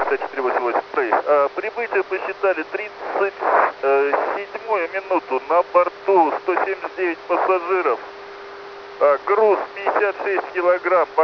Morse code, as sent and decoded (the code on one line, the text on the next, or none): none